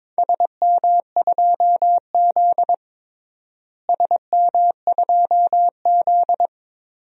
Morse code, heard as a keyed tone, about 22 wpm, 700 Hz. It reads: SM2Z SM2Z
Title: DXCC-100 Call Signs - 4 Characters - Repeated 3x - 22wpm